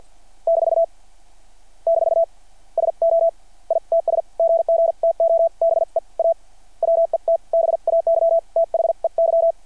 Morse code, heard as a keyed tone, about 48 wpm, spaced as for 42 wpm, 660 Hz. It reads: - - SO ITS GG TO BE A WET BAY THEY
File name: archived / CRSnet